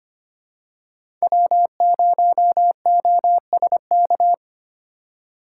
Morse code, heard as a keyed tone, about 25 wpm, 700 Hz. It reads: W0OSK
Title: USA Call Signs - 3x Character Spacing - 25wpm